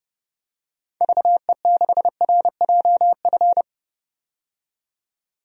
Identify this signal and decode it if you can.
Morse code, heard as a keyed tone, about 30 wpm, 700 Hz. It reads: VE6RJF
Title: DXCC-100 Call Signs - 6 Characters - 30wpm